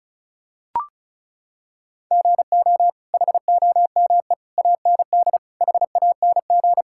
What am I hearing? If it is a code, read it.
Morse code, heard as a keyed tone, about 36 wpm, 700 Hz. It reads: GO HOME AND HANG
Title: Sentences from Top 1600 Words 35wpm